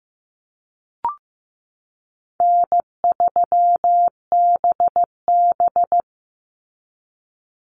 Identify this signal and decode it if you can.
Morse code, heard as a keyed tone, about 15 wpm, 700 Hz. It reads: N3BB
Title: Easy Call Signs Repeated 2x - 15wpm